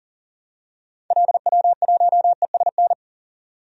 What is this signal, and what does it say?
Morse code, heard as a keyed tone, about 40 wpm, 700 Hz. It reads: LW1ESN
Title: DXCC-100 Call Signs - 6 Characters - 40wpm